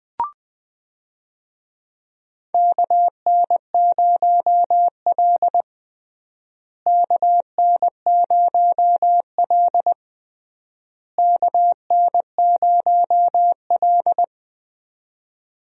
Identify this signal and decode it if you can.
Morse code, heard as a keyed tone, about 21 wpm, 700 Hz. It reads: KN0L KN0L KN0L
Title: USA Call Signs - 4 Characters - Repeated 3x - 20wpm